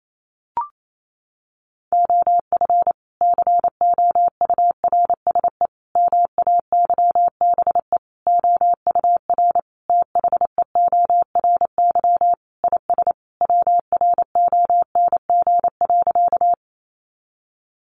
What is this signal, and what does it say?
Morse code, heard as a keyed tone, about 28 wpm, 700 Hz. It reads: OF COURSE MAYBE OUR THEORY IS WRONG.
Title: A Sound Of Thunder 28wpm